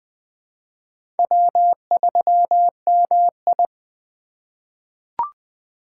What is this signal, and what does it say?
Morse code, heard as a keyed tone, about 20 wpm, 700 Hz. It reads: W3MI
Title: USA Call Signs - 2x Character Spacing - 20wpm